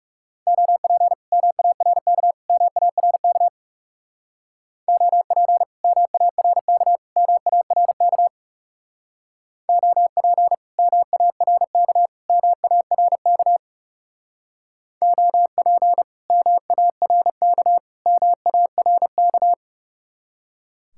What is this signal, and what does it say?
Morse code, heard as a keyed tone, about 39 wpm, 700 Hz. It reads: OP MARK MARK OP MARK MARK OP MARK MARK OP MARK MARK